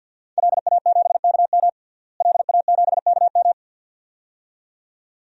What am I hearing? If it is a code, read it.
Morse code, heard as a keyed tone, about 51 wpm, 700 Hz. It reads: LA7KM LA7KM
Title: DXCC-100 Call Signs - 5 Characters - Repeated 2x - 50wpm